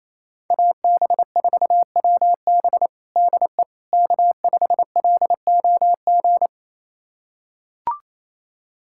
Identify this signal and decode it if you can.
Morse code, heard as a keyed tone, about 28 wpm, 700 Hz. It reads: AB4WB DE K5LOG
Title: USA Callsign DE Callsign - 5 Characters - 28wpm